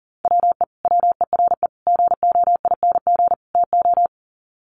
Morse code, heard as a keyed tone, about 40 wpm, 700 Hz. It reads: WE WERE GOING TO
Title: Sets of 4 Words 40wpm